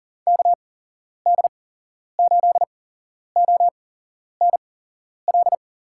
Morse code, heard as a keyed tone, about 40 wpm, spaced as for 11 wpm, 700 Hz. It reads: KD8ONL
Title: USA Call Signs - 6 Characters - 8x Character Spacing - 40wpm